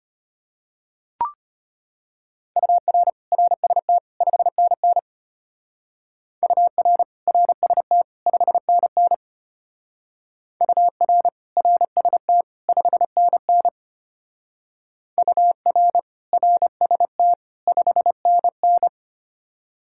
Morse code, heard as a keyed tone, about 32 wpm, 700 Hz. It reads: UR RST 5NN UR RST 5NN UR RST 5NN UR RST 5NN